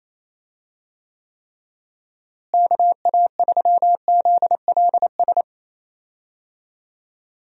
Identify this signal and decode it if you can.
Morse code, heard as a keyed tone, about 28 wpm, 700 Hz. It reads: KA3ZLS
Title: USA Call Signs - 6 Characters - Repeated 3x - 28wpm